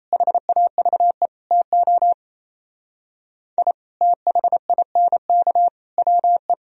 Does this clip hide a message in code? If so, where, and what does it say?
Yes: Morse code from 0.1 to 6.5 s, heard as a keyed tone, about 37 wpm, 700 Hz. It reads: HAVE TO I THINK WE